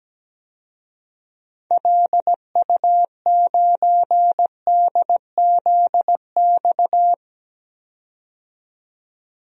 Morse code, heard as a keyed tone, about 17 wpm, 700 Hz. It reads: LU9DZX
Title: DXCC-100 Call Signs - 6 Characters - 17wpm